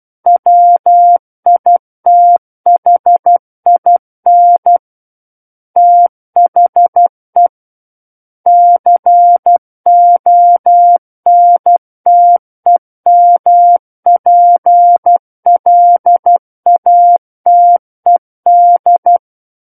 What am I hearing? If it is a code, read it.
Morse code, heard as a keyed tone, about 12 wpm, 700 Hz. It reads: WITHIN THE CONTEMPLATED
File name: Book One / 03-Chapter 3 - 12 WPM